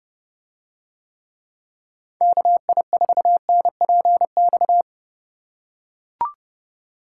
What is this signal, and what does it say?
Morse code, heard as a keyed tone, about 27 wpm, 700 Hz. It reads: KI4NPX